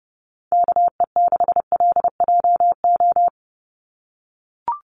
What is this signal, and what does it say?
Morse code, heard as a keyed tone, about 30 wpm, 700 Hz. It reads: KE6LJO
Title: USA Call Signs 30wpm